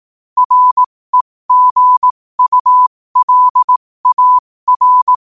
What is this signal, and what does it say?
Morse code, heard as a keyed tone, about 18 wpm, spaced as for 15 wpm, 1000 Hz. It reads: REGULAR
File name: qotdfiles / 15wpm